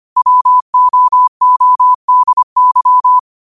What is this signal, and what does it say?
Morse code, heard as a keyed tone, about 25 wpm, 1000 Hz. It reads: WOODY
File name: qotdfiles / 25wpm